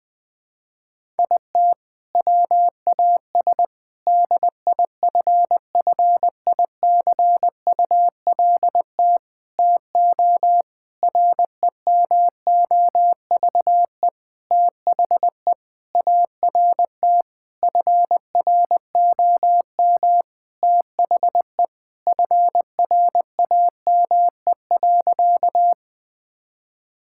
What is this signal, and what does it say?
Morse code, heard as a keyed tone, about 20 wpm, 700 Hz. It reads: IT WAS DIFFICULT TO REMOVE THE ART FROM THE FRAME.
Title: Sentences from Top 1000 Words - Review Part 6 - 20wpm